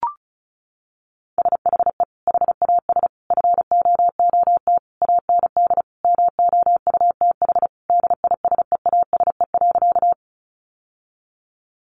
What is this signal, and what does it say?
Morse code, heard as a keyed tone, about 34 wpm, 700 Hz. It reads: SHE HAS FOOT AND MOUTH DISEASE.
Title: Sentences from Top 1500 Words 35wpm